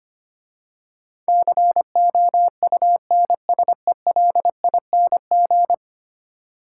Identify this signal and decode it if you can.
Morse code, heard as a keyed tone, about 25 wpm, 700 Hz. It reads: COUNSELING
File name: 10 Letter Words - First 3 Letters Given - 25wpm